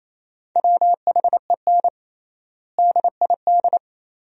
Morse code, heard as a keyed tone, about 28 wpm, 700 Hz. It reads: WHEN DID
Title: 3x Word-Spacing - Sentences from Top 300 Words - 28wpm